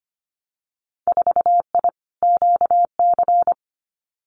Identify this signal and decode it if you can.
Morse code, heard as a keyed tone, about 25 wpm, 700 Hz. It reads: 4I QC
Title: Winter Field Day Exchanges Encore 25wpm